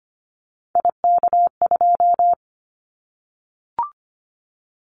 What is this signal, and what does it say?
Morse code, heard as a keyed tone, about 25 wpm, 700 Hz. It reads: IK2
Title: Callsign Prefixes for Top 25 DXCC - Repeated 3x - 25wpm